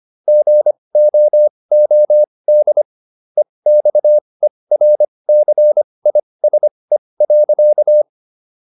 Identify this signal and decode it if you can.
Morse code, heard as a keyed tone, about 25 wpm, spaced as for 20 wpm, 600 Hz. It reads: GOOD EXERCISE.